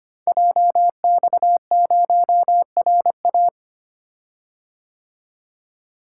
Morse code, heard as a keyed tone, about 25 wpm, 700 Hz. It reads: JX0RA
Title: Code Groups - 5 Letter:Numbers - 25wpm